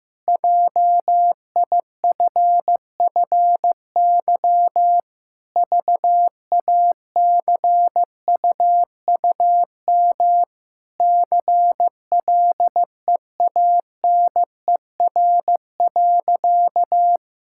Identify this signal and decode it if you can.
Morse code, heard as a keyed tone, about 15 wpm, 700 Hz. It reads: JIFFY VACUUM CLEANER.
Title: Pangram Challenge v2 - 15wpm